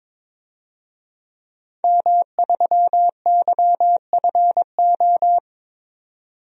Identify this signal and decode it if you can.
Morse code, heard as a keyed tone, about 22 wpm, 700 Hz. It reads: M3YFO
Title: DXCC-100 Call Signs - 5 Characters - 22wpm